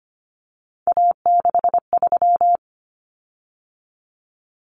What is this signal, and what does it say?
Morse code, heard as a keyed tone, about 25 wpm, 700 Hz. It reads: A63
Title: Common Cut Numbers - 3-Digit 25wpm